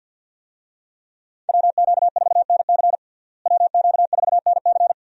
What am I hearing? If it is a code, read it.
Morse code, heard as a keyed tone, about 47 wpm, 700 Hz. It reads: WQ4NC WQ4NC
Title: USA Call Signs - 5 Characters - Repeated 3x - 50wpm